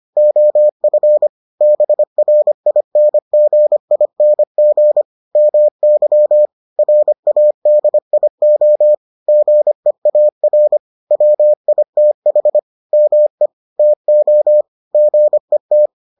Morse code, heard as a keyed tone, about 25 wpm, 600 Hz. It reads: OF BRINGING MY RADIO GEAR WITH ME TO GET